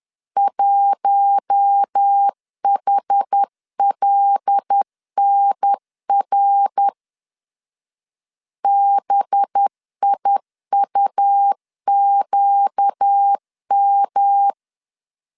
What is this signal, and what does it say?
Morse code, heard as a keyed tone, about 10 wpm, 800 Hz. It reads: 1HLNR BIUQM
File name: CW / DET / 1 6 1500C S B 10wpm 4137